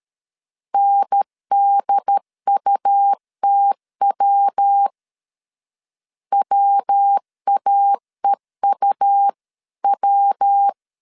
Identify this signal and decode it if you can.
Morse code, heard as a keyed tone, about 13 wpm, 800 Hz. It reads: NDUTW WAEUW
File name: CW / DET / 1 3 1500C S B 12wpm 2828